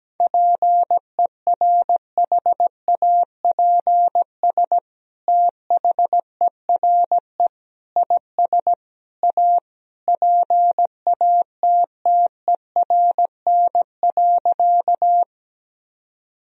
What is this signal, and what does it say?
Morse code, heard as a keyed tone, about 17 wpm, 700 Hz. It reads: PERHAPS THERE IS A PATTERN.
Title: Sentences from Top 1000 Words - Review Part 2 - 17wpm